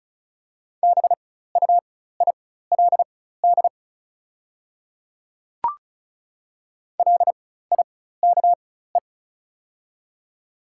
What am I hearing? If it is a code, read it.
Morse code, heard as a keyed tone, about 34 wpm, spaced as for 16 wpm, 700 Hz. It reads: BUILD LIKE
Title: 4x Character Spacing - Top 200 Words - 35wpm